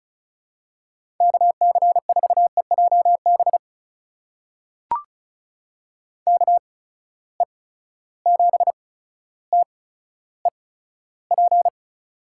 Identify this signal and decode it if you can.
Morse code, heard as a keyed tone, about 36 wpm, spaced as for 15 wpm, 700 Hz. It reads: KC4EJB KE7TEP
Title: USA Call Signs - 8x Character Spacing - 35wpm